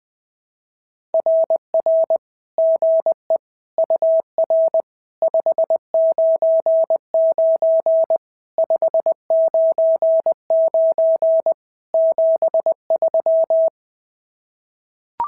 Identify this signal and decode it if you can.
Morse code, heard as a keyed tone, about 20 wpm, 650 Hz. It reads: RR GE UR 599 599 73
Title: SOTA QSO Practice 20wpm